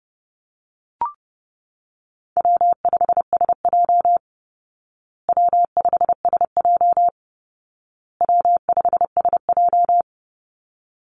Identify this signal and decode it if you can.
Morse code, heard as a keyed tone, about 30 wpm, 700 Hz. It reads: W5SJ W5SJ W5SJ
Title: Call Signs Repeated 2x - 30wpm